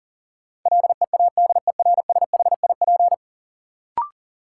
Morse code, heard as a keyed tone, about 40 wpm, 700 Hz. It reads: LEADERSHIP
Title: Sentences from Top 1600 Words - New Word Drill - 40wpm